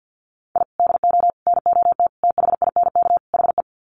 Morse code, heard as a keyed tone, about 50 wpm, 700 Hz. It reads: I DO NOT THINK HE
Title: Sets of 5 Words 50wpm